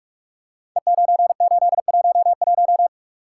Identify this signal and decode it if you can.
Morse code, heard as a keyed tone, about 44 wpm, 700 Hz. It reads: E9811
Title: Code Groups - 5 Letter:Numbers - 45wpm